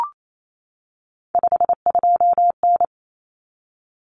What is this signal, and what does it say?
Morse code, heard as a keyed tone, about 28 wpm, 700 Hz. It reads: 52N
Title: Common Cut Numbers - 3-Digit 28wpm